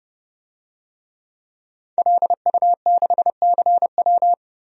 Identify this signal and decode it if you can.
Morse code, heard as a keyed tone, about 30 wpm, 700 Hz. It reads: LU6CW
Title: DXCC-100 Call Signs - 5 Characters - Repeated 2x - 30wpm